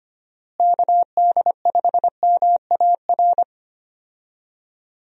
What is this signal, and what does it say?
Morse code, heard as a keyed tone, about 25 wpm, 700 Hz. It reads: KD5MAR
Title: USA Call Signs - 6 Characters - Repeated 3x - 25wpm